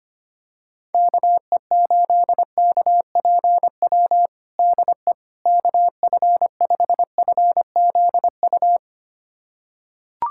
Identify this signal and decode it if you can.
Morse code, heard as a keyed tone, about 25 wpm, 700 Hz. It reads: KE8KPW DE KF5FZU
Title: USA Callsign DE Callsign 25wpm